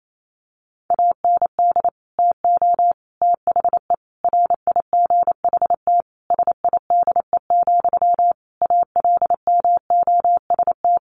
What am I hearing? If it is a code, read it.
Morse code, heard as a keyed tone, about 28 wpm, 700 Hz. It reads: AND TO THE RIGHT SIDE, ALMOST
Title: Sentences from Top 1300 Words 28wpm